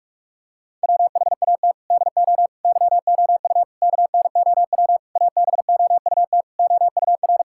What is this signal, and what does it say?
Morse code, heard as a keyed tone, about 46 wpm, 700 Hz. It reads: WHAT DO YOU KNOW ABOUT OUR
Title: Sentences from Top 1000 Words - Review Part 6 - 45wpm